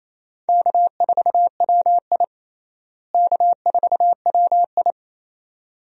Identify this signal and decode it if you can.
Morse code, heard as a keyed tone, about 28 wpm, 700 Hz. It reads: K4WI K4WI
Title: USA Call Signs - 4 Characters - Repeated 2x - 28wpm